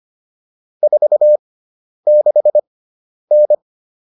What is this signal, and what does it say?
Morse code, heard as a keyed tone, about 25 wpm, spaced as for 10 wpm, 600 Hz. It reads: 46N
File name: QSO306 10